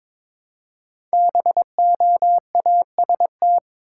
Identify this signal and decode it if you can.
Morse code, heard as a keyed tone, about 22 wpm, 700 Hz. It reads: BOAST
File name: 5 Letter Words - First 3 Letters Given - 22wpm